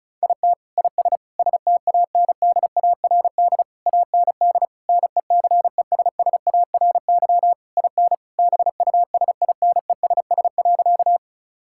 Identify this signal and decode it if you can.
Morse code, heard as a keyed tone, about 34 wpm, 700 Hz. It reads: IT IS STANDARD AND NECESSARY IN BUSINESS.
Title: Sentences from Top 1000 Words - Review Part 6 - 35wpm